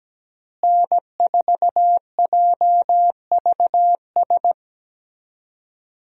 Morse code, heard as a keyed tone, about 17 wpm, 700 Hz. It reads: N4JVS
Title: USA Call Signs - 5 Characters - Repeated 3x - 17wpm